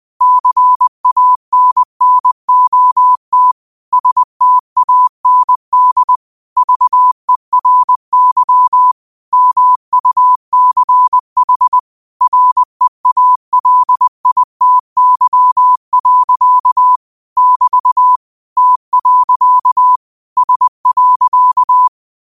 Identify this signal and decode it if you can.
Morse code, heard as a keyed tone, about 20 wpm, 1000 Hz. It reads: CANNOT STAND VERY MUCH REALITY. <BT> T. S.
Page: Quotes for Thu, 14 Aug 2025 in Morse Code at 20 words per minute.